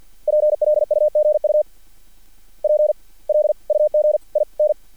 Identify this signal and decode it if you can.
Morse code, heard as a keyed tone, about 50 wpm, spaced as for 29 wpm, 600 Hz. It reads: PLUGUP LUG IN
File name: archived / CRSnet